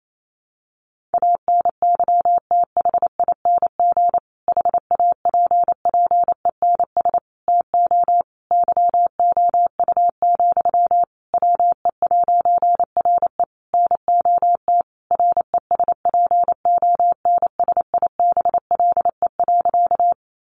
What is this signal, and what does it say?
Morse code, heard as a keyed tone, about 28 wpm, 700 Hz. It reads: ANYTHING HAPPENS TO YOU, WE'RE NOT RESPONSIBLE.
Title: A Sound Of Thunder 28wpm